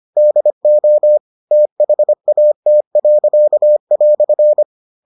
Morse code, heard as a keyed tone, about 25 wpm, 600 Hz. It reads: DO THAT."